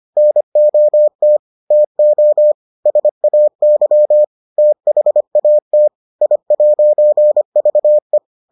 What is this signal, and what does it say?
Morse code, heard as a keyed tone, about 25 wpm, 600 Hz. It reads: NOT TO SAY THAT I'VE